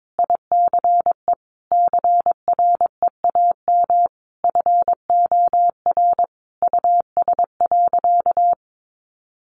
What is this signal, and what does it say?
Morse code, heard as a keyed tone, about 22 wpm, 700 Hz. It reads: ICE CREAM FOR US.
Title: Sentences from Top 1500 Words 22wpm